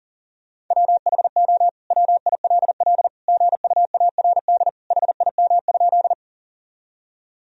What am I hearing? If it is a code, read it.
Morse code, heard as a keyed tone, about 40 wpm, 700 Hz. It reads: WHO WILL GUARD HIM?
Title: Sentences from Top 1000 Words - Review Part 4 - 40wpm